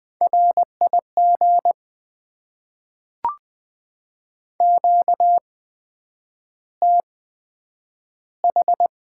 Morse code, heard as a keyed tone, about 20 wpm, 700 Hz. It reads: R I G Q T H
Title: Top 20 QSO Elements - 8x Character Spacing - 20wpm